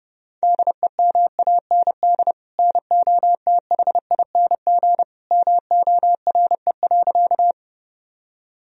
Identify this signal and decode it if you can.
Morse code, heard as a keyed tone, about 32 wpm, 700 Hz. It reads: DEMAND NOTHING MORE.
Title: Sentences from Top 1000 Words - Review Part 4 - 30wpm